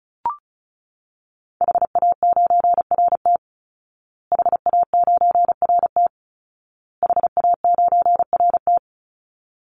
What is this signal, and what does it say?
Morse code, heard as a keyed tone, about 34 wpm, 700 Hz. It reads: HA9RT HA9RT HA9RT
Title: Call Signs Repeated 2x - 35wpm